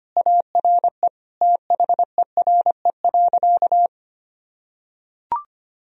Morse code, heard as a keyed tone, about 25 wpm, 700 Hz. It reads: ARE THERE.
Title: Sentences from Top 1000 Words - Review Part 1 - 25wpm